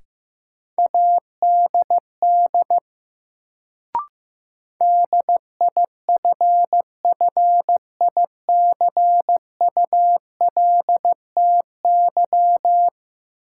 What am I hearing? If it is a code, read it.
Morse code, heard as a keyed tone, about 15 wpm, 700 Hz. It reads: ADD DIFFICULTY
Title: Sentences from Top 1500 Words - New Word Drill Alt - 15wpm